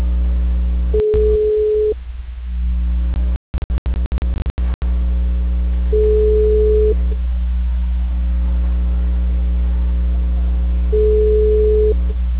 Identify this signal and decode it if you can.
Ringback tone (Europe, 425 Hz).